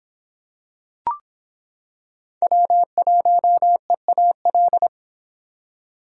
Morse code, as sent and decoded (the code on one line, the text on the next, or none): .-- .---- . .- .-..
W1EAL